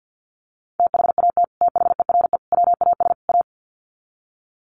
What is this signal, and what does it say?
Morse code, heard as a keyed tone, about 50 wpm, 700 Hz. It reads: THAT THERE WAS A
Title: Sets of 4 Words 50wpm